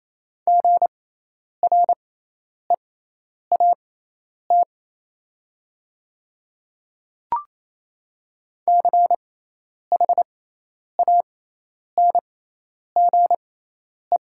Morse code, heard as a keyed tone, about 28 wpm, spaced as for 10 wpm, 700 Hz. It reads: GREAT CHANGE
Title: 6x Character Spacing - Top 200 Words - 28wpm